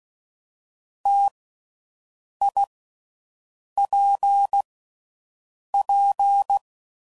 Morse code, heard as a keyed tone, about 16 wpm, 800 Hz. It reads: T I P P
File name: Combinations / Difficult Combos / T AOEI's